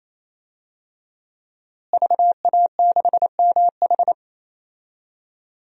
Morse code, heard as a keyed tone, about 28 wpm, 690 Hz. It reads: VA6MH
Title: DXCC-100 Call Signs - 5 Characters - Repeated 3x - 28wpm